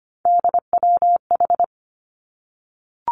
Morse code, heard as a keyed tone, about 25 wpm, 700 Hz. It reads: DWH
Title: Three Challenging Letter-Number Combinations 25wpm